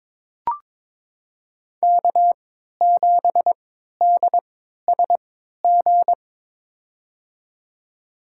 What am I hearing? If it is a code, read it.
Morse code, heard as a keyed tone, about 22 wpm, spaced as for 13 wpm, 700 Hz. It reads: K7DSG
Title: USA Call Signs - 5 Characters - 3x Character Spacing - 22wpm